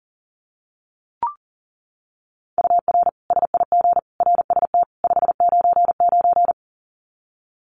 Morse code, heard as a keyed tone, about 40 wpm, 700 Hz. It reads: UR SIG RST 599
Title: Common QSO Phrases 40wpm